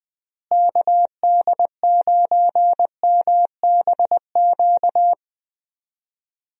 Morse code, heard as a keyed tone, about 20 wpm, 700 Hz. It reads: KD9MBQ
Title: USA Call Signs - 6 Characters - Repeated 2x - 20wpm